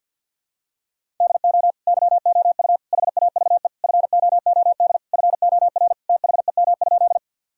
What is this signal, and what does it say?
Morse code, heard as a keyed tone, about 52 wpm, 700 Hz. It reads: DO YOU HAVE FOOD FOR THEM?
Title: Sentences from Top 1000 Words - Review Part 1 - 50wpm